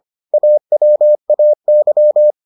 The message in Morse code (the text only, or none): AWAY